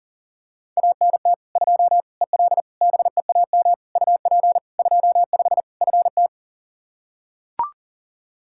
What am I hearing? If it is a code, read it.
Morse code, heard as a keyed tone, about 40 wpm, 700 Hz. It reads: ANT 2 EL BEAM UP 25 FT
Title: K1USN SST QSO Practice 40wpm